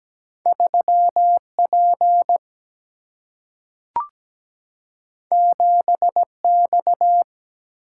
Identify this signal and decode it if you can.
Morse code, heard as a keyed tone, about 17 wpm, 700 Hz. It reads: 3P 7X
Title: Single-Letter Single-Number 17wpm